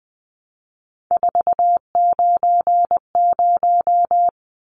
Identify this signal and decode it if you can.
Morse code, heard as a keyed tone, about 20 wpm, 700 Hz. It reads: 490
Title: Numbers - 3-Digit 20wpm